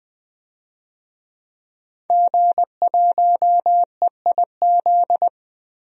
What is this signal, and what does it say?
Morse code, heard as a keyed tone, about 21 wpm, 700 Hz. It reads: G1EIZ
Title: DXCC-100 Call Signs - 5 Characters - 20wpm